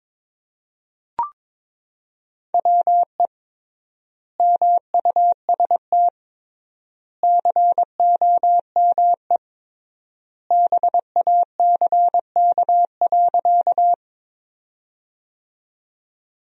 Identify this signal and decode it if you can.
Morse code, heard as a keyed tone, about 21 wpm, 700 Hz. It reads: WE MUST COME BACK.
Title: 3x Word-Spacing - Sentences from Top 200 Words - 22wpm